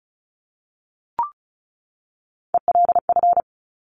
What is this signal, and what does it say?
Morse code, heard as a keyed tone, about 36 wpm, 700 Hz. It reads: ELF
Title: Top Three Letter Combinations in Words 35wpm